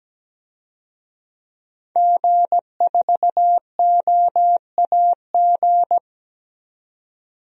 Morse code, heard as a keyed tone, about 17 wpm, 700 Hz. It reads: G4OAG
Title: DXCC-100 Call Signs - 5 Characters - 17wpm